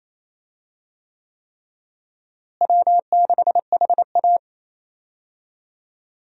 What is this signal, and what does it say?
Morse code, heard as a keyed tone, about 27 wpm, 700 Hz. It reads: W6HA
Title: USA Call Signs - 4 Characters - Repeated 3x - 28wpm